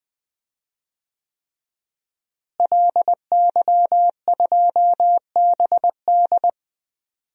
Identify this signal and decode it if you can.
Morse code, heard as a keyed tone, about 21 wpm, 700 Hz. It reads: LY2BD
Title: DXCC-100 Call Signs - 5 Characters - Repeated 3x - 20wpm